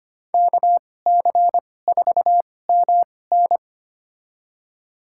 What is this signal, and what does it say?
Morse code, heard as a keyed tone, about 25 wpm, spaced as for 18 wpm, 700 Hz. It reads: KC4MN
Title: USA Call Signs - 5 Characters - 2x Character Spacing - 25wpm